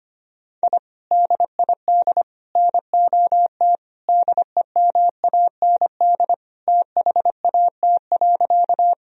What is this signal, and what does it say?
Morse code, heard as a keyed tone, about 25 wpm, 700 Hz. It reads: I DID NOT DEMAND THAT.
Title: Sentences from Top 1000 Words - Review Part 4 - 25wpm